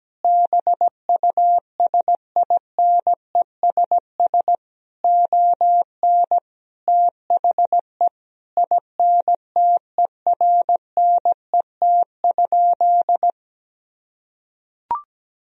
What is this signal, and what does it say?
Morse code, heard as a keyed tone, about 17 wpm, 700 Hz. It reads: BUSINESS ON THE INTERNET?
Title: Sentences from Top 1000 Words - Review Part 4 - 17wpm